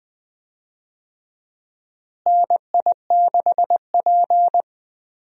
Morse code, heard as a keyed tone, about 20 wpm, 700 Hz. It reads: NI6P
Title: USA Call Signs - 4 Characters - Repeated 2x - 20wpm